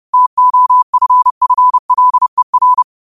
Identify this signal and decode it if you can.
Morse code, heard as a keyed tone, about 30 wpm, 1000 Hz. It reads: TOFFLER
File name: qotdfiles / 30wpm